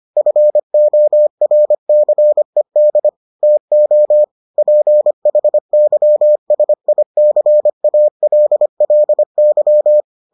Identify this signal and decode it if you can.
Morse code, heard as a keyed tone, about 25 wpm, 600 Hz. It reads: FORCED TO PHYSICALLY